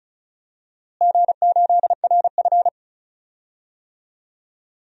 Morse code, heard as a keyed tone, about 34 wpm, 700 Hz. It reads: G8RF
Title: DXCC-100 Call Signs - 4 Characters - Repeated 3x - 35wpm